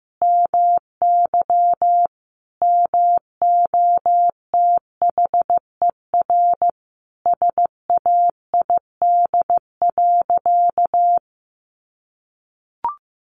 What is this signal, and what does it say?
Morse code, heard as a keyed tone, about 15 wpm, 700 Hz. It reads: MY MOTHER SAID.
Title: Sentences from Top 700 Words 15wpm